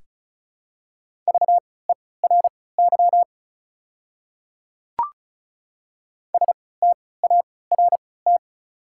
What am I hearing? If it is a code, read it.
Morse code, heard as a keyed tone, about 34 wpm, spaced as for 20 wpm, 690 Hz. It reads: VERY START
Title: 3x Character Spacing - Top 300 Words - 35wpm